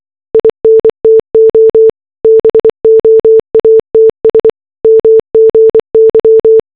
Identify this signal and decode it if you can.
Morse code, heard as a keyed tone, about 24 wpm, 440 Hz. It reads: INTO BOATS MGY
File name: RMS Titanic distress signal simulated as morse code